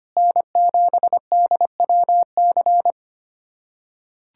Morse code, heard as a keyed tone, about 25 wpm, 700 Hz. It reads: N7DWC
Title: USA Callsign DE Callsign 25wpm